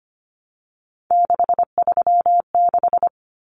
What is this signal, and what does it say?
Morse code, heard as a keyed tone, about 25 wpm, 700 Hz. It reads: 636
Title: Numbers - 3-Digit 25wpm